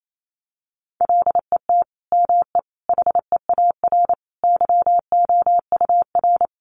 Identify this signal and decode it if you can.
Morse code, heard as a keyed tone, about 28 wpm, 700 Hz. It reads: LET ME HEAR YOUR
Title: Sentences from Top 300 Words 28wpm